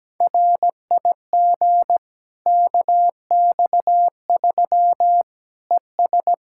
Morse code, heard as a keyed tone, about 17 wpm, 700 Hz. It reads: RIG KX3 ES
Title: Practice QSO - Exchange 2 - Info - 17wpm